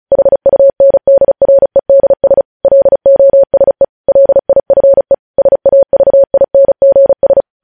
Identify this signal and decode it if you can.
Morse code, heard as a keyed tone, about 36 wpm, 550 Hz. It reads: HUNDREDS LOSE LIFE SAVINGS
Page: News headlines in Morse code at 35 wpm